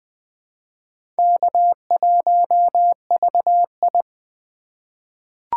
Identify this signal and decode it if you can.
Morse code, heard as a keyed tone, about 20 wpm, 700 Hz. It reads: K1VI
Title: USA Call Signs - 4 Characters - 4x Character Spacing - 20wpm